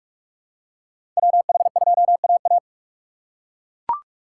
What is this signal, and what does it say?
Morse code, heard as a keyed tone, about 42 wpm, 700 Hz. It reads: WH2AA
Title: USA Call Signs - 6x Character Spacing - 45wpm